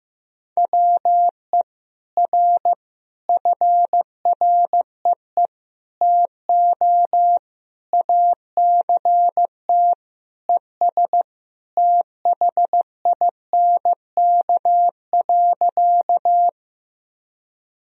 Morse code, heard as a keyed tone, about 15 wpm, 700 Hz. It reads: WE R FREE TO ACT ES THINK.
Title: Sentences from Top 500 Words and Abbreviations 15wpm